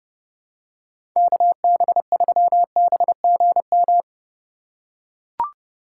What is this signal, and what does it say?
Morse code, heard as a keyed tone, about 30 wpm, 700 Hz. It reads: KB3BGM